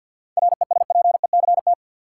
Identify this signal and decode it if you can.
Morse code, heard as a keyed tone, about 50 wpm, 700 Hz. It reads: RESPECT
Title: Seven Letter Words - 50wpm